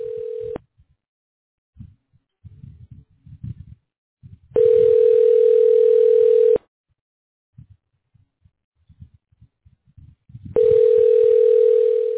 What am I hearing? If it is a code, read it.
Ringback tone (North America), 440 + 480 Hz.